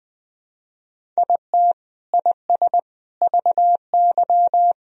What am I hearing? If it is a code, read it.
Morse code, heard as a keyed tone, about 21 wpm, 700 Hz. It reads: IT IS VY
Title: Sentences from Top 200 Words and Abbreviations 20wpm